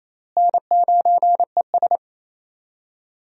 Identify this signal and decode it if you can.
Morse code, heard as a keyed tone, about 28 wpm, 700 Hz. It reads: N9ES
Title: USA Call Signs - 4 Characters - Repeated 3x - 28wpm